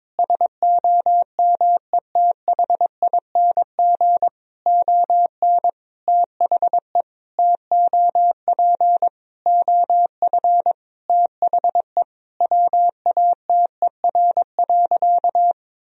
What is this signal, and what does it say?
Morse code, heard as a keyed tone, about 22 wpm, 700 Hz. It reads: SOMETHING ON THE TOP OF THE WATER.
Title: Sentences from Top 1000 Words - Review Part 2 - 22wpm